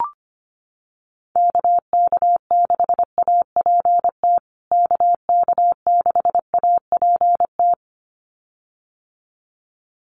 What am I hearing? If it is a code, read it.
Morse code, heard as a keyed tone, about 25 wpm, 700 Hz. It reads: KK6APT KK6APT
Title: USA Call Signs Repeated 2x 25wpm